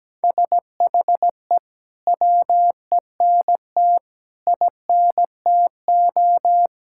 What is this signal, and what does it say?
Morse code, heard as a keyed tone, about 17 wpm, 700 Hz. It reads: SHE WENT INTO
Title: Sentences from Top 1000 Words - Review Part 6 - 17wpm